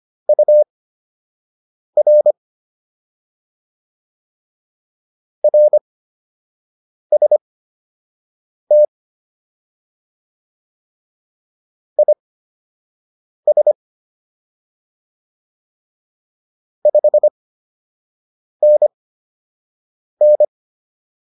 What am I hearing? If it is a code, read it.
Morse code, heard as a keyed tone, about 25 wpm, spaced as for 6 wpm, 600 Hz. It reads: UR RST IS 5NN